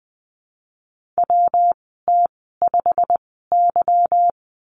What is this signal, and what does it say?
Morse code, heard as a keyed tone, about 20 wpm, spaced as for 15 wpm, 700 Hz. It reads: WT5Y
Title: 179 Lesson 29g - Slash - Call Signs 20wpm